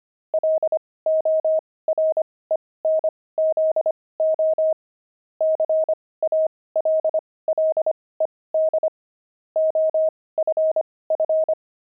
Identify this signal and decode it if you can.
Morse code, heard as a keyed tone, about 25 wpm, spaced as for 18 wpm, 630 Hz. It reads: LORENZO CALLED OFF
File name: pangram-fs-18-cs-25-kidding